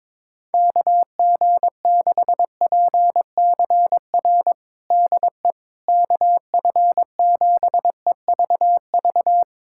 Morse code, heard as a keyed tone, about 22 wpm, 700 Hz. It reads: KG6PCR DE KF7EVV
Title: USA Callsign DE Callsign - 6 Characters - 22wpm